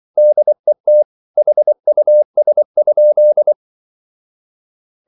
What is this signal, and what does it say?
Morse code, heard as a keyed tone, about 24 wpm, 600 Hz. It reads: DET HUS?
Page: Ferie 24 wpm